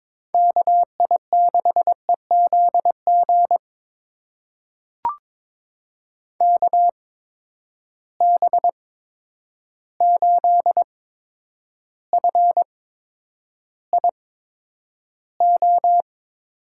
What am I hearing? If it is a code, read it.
Morse code, heard as a keyed tone, about 22 wpm, spaced as for 9 wpm, 700 Hz. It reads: KI6EZG KB8FIO
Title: USA Call Signs - 6 Characters - 8x Character Spacing - 22wpm